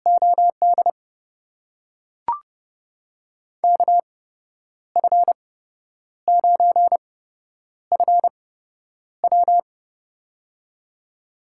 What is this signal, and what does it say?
Morse code, heard as a keyed tone, about 29 wpm, spaced as for 10 wpm, 700 Hz. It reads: OD KF9FW
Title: USA Call Signs - 5 Characters - 8x Character Spacing - 30wpm